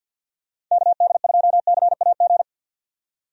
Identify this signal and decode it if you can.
Morse code, heard as a keyed tone, about 46 wpm, 700 Hz. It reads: KD2CAG
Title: USA Call Signs - 6 Characters - 50wpm